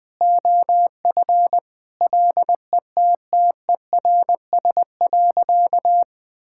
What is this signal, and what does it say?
Morse code, heard as a keyed tone, about 20 wpm, 700 Hz. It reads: OF LETTERS.
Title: Sentences from Top 1000 Words - Review Part 6 - 20wpm